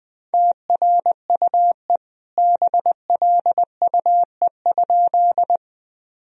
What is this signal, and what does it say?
Morse code, heard as a keyed tone, about 20 wpm, 700 Hz. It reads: TRUE BLUE?
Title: Sentences from Top 1000 Words - Review Part 3 - 20wpm